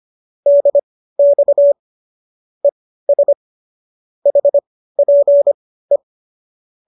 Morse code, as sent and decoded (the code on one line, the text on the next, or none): -.. -..- / . ... / .... .--. .
DX ES HPE